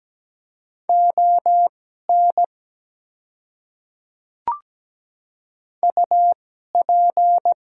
Morse code, heard as a keyed tone, about 17 wpm, spaced as for 12 wpm, 700 Hz. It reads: ON UP
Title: 2x Character Spacing - Top 100 Words - 17wpm